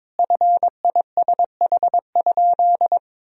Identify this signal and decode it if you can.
Morse code, heard as a keyed tone, about 22 wpm, 700 Hz. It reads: FISH?
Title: Sentences from Top 1000 Words - Review Part 1 - 22wpm